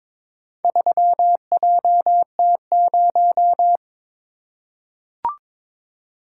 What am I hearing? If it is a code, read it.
Morse code, heard as a keyed tone, about 22 wpm, 700 Hz. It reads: 3JT0
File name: practice / 22wpm / Four Letter-Number Combinations - 22wpm